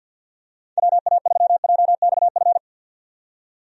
Morse code, heard as a keyed tone, about 46 wpm, 700 Hz. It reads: WA3JXF
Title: USA Call Signs - 6 Characters - 50wpm